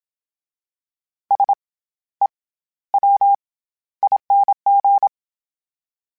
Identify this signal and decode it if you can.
Morse code, heard as a keyed tone, about 27 wpm, 800 Hz. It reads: S E W ING
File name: Special INGs-mp3s / Special INGs-mp3s / SlowToFaster ing @25